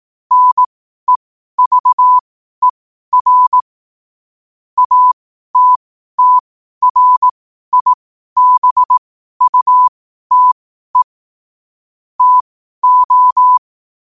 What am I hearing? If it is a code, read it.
Morse code, heard as a keyed tone, about 18 wpm, spaced as for 12 wpm, 1000 Hz. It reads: NEVER ATTRIBUTE TO
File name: qotdfiles / 12wpm